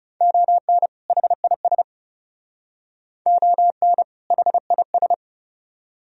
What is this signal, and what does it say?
Morse code, heard as a keyed tone, about 40 wpm, 700 Hz. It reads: ON HIS ON HIS